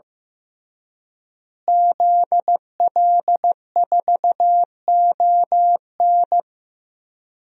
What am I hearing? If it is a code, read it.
Morse code, heard as a keyed tone, about 15 wpm, 700 Hz. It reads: ZL4ON
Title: DXCC-100 Call Signs - 5 Characters - Repeated 3x - 15wpm